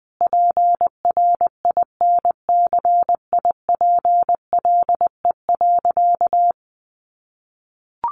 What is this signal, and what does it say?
Morse code, heard as a keyed tone, about 20 wpm, 700 Hz. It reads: PRINCIPLE.
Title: Sentences from Top 1400 Words 20wpm